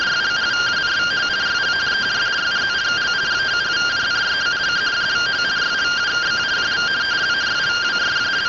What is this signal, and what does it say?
Two-tone teleprinter signal, about 170 Hz shift.
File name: findings / CT 1 Scout 0002.gen-43 rtty rx